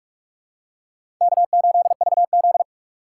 Morse code, heard as a keyed tone, about 46 wpm, 700 Hz. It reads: K8VZ
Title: USA Call Signs - 4 Characters - 4x Character Spacing - 45wpm